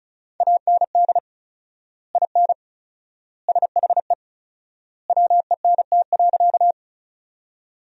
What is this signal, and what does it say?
Morse code, heard as a keyed tone, about 34 wpm, 700 Hz. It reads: AND IN SHE WENT.
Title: 4x Word-Spacing - Sentences from Top 200 Words - 35wpm